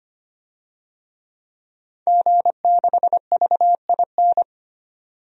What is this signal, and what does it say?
Morse code, heard as a keyed tone, about 25 wpm, 700 Hz. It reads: G6VIN
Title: DXCC-100 Call Signs - 5 Characters - 25wpm